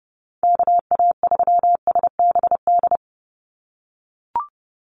Morse code, heard as a keyed tone, about 30 wpm, 700 Hz. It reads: KA3SBD
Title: USA Call Signs 30wpm